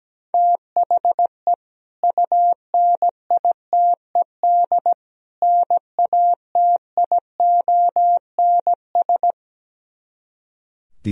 Morse code, heard as a keyed tone, about 17 wpm, 700 Hz. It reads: THE UNITED NATIONS
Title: Speed-Racing - Sets of 3 Words 17wpm